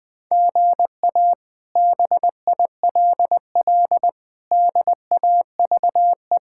Morse code, heard as a keyed tone, about 21 wpm, 700 Hz. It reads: GA BILL DAVE
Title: K1USN SST QSO Practice - 20wpm